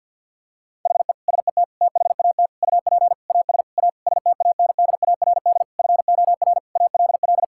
Morse code, heard as a keyed tone, about 50 wpm, 700 Hz. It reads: HE SET THAT UP AS A STANDARD FOR ALL
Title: Sentences from Top 1000 Words - Review Part 6 - 50wpm